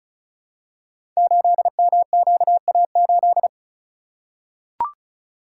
Morse code, heard as a keyed tone, about 34 wpm, 700 Hz. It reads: 8MQA8
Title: Code Groups - 5 Letter:Numbers - 35wpm